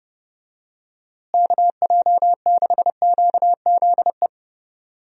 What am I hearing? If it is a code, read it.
Morse code, heard as a keyed tone, about 30 wpm, 700 Hz. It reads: KJ6QZE
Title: USA Call Signs - 6 Characters - Repeated 2x - 30wpm